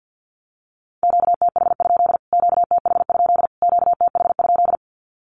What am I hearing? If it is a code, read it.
Morse code, heard as a keyed tone, about 50 wpm, 700 Hz. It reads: QTH? QTH? QTH?
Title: Q-Codes Repeated 3x 50wpm